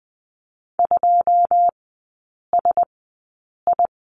Morse code, heard as a keyed tone, about 20 wpm, spaced as for 8 wpm, 700 Hz. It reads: 2SI
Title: 081 Lesson 14e - Number 2 - Three Characters 20wpm